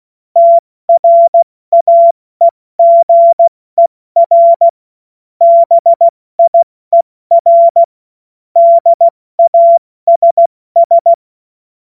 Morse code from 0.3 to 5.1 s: TRAEGER